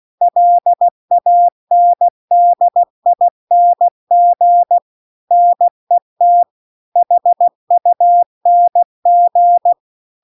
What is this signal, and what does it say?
Morse code, heard as a keyed tone, about 16 wpm, 700 Hz. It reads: LANDING NET HUNG
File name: CW-The-Big-Twoheaded-River-2-16WPM